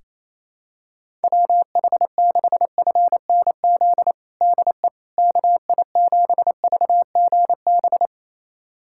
Morse code, heard as a keyed tone, about 28 wpm, 700 Hz. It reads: WH6FNZ DE KI7VGB
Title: USA Callsign DE Callsign - 6 Characters - 28wpm